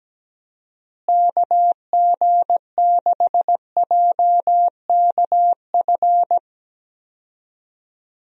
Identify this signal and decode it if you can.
Morse code, heard as a keyed tone, about 17 wpm, 700 Hz. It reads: KG6JKF
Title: USA Call Signs - 6 Characters - Repeated 2x - 17wpm